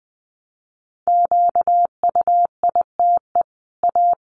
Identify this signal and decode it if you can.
Morse code, heard as a keyed tone, about 20 wpm, 700 Hz. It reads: QUITE A
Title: Sentences from Top 1200 Words 20wpm